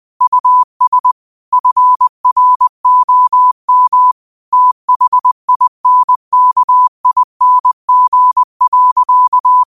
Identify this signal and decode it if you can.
Morse code, heard as a keyed tone, about 20 wpm, 1000 Hz. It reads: US FROM THINKING.
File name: qotdfiles / 20wpm